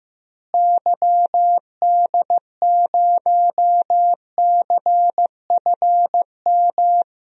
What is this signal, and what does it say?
Morse code, heard as a keyed tone, about 15 wpm, 700 Hz. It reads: YD0CFM
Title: DXCC-100 Call Signs - 6 Characters - Repeated 3x - 15wpm